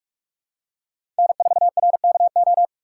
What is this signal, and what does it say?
Morse code, heard as a keyed tone, about 46 wpm, 700 Hz. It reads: N4RKO
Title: USA Call Signs - 5 Characters - 6x Character Spacing - 45wpm